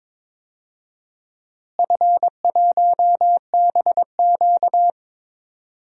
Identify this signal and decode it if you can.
Morse code, heard as a keyed tone, about 22 wpm, 700 Hz. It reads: F1BQ
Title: DXCC-100 Call Signs - 4 Characters - Repeated 2x - 22wpm